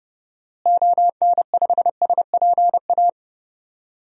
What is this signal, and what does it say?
Morse code, heard as a keyed tone, about 30 wpm, 700 Hz. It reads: ON5SPA
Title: DXCC-100 Call Signs - 6 Characters - Repeated 2x - 30wpm